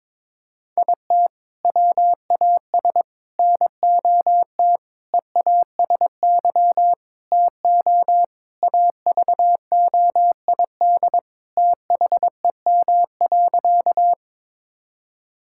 Morse code, heard as a keyed tone, about 22 wpm, 700 Hz. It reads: IT WAS NOT EASY TO AVOID THEM.
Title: Sentences from Top 1000 Words - Review Part 6 - 22wpm